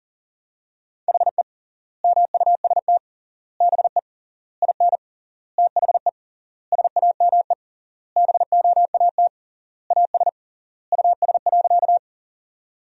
Morse code, heard as a keyed tone, about 40 wpm, 700 Hz. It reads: HE MUST BE IN THE SAME BOAT AS US.
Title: 3x Word-Spacing - Sentences from Top 500 Words - 40wpm